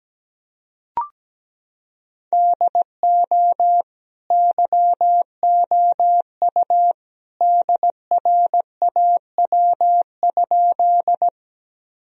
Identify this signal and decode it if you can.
Morse code, heard as a keyed tone, about 17 wpm, 700 Hz. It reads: DO YOU DRAW?
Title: Sentences from Top 1000 Words - Review Part 1 - 17wpm